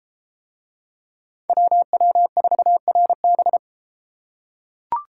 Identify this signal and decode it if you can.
Morse code, heard as a keyed tone, about 33 wpm, 700 Hz. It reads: WW4RB